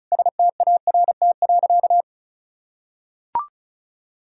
Morse code, heard as a keyed tone, about 36 wpm, 700 Hz. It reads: START.
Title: Sentences from Top 300 Words and Abbreviations 35wpm